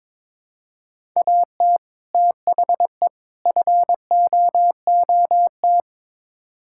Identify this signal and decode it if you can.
Morse code, heard as a keyed tone, about 22 wpm, 700 Hz. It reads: AT THE FOOT
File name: practice / 22wpm / Build A Phrase - 5-Word Phrases - 22wpm